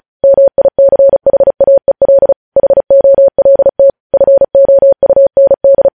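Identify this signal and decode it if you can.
Morse code, heard as a keyed tone, about 36 wpm, 550 Hz. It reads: MICHAEL HOLT FOUND
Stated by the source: News headlines sent in Morse at 35 WPM.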